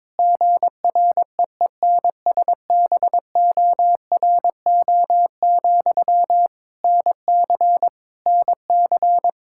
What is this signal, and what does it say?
Morse code, heard as a keyed tone, about 22 wpm, 700 Hz. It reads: GREENSBORO, NC NC
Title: Practice QSO - Exchange 1 - Beginning - 22wpm